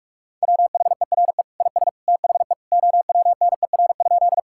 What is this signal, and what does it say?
Morse code, heard as a keyed tone, about 46 wpm, 700 Hz. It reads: WHERE IS THE OWNER?
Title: Sentences from Top 1600 Words 45wpm